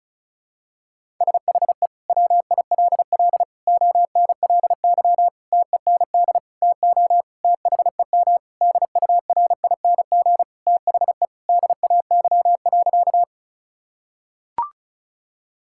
Morse code, heard as a keyed tone, about 36 wpm, 700 Hz. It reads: SHE WILL ONLY TEND TO THEM DURING THE DAY.
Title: Sentences from Top 1000 Words - Review Part 5 - 35wpm